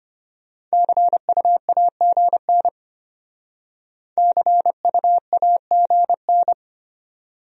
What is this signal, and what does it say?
Morse code, heard as a keyed tone, about 32 wpm, 700 Hz. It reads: CUAGN CUAGN